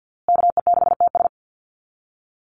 Morse code, heard as a keyed tone, about 50 wpm, 700 Hz. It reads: KE6TS
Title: USA Call Signs Repeated 3x 50wpm